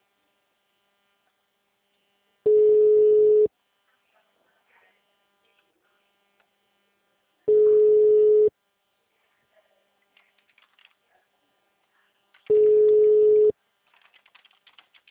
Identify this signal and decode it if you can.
Ringback tone (Europe, 425 Hz).
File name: out- -2015- - - .18